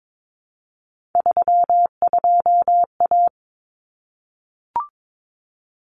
Morse code, heard as a keyed tone, about 22 wpm, 700 Hz. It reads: 32A
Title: Common Cut Numbers - 3-Digit 22wpm